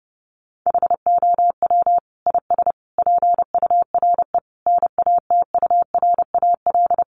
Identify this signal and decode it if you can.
Morse code, heard as a keyed tone, about 30 wpm, 700 Hz. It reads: HOW IS PURE NATURAL
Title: Sentences from Top 1500 Words 30wpm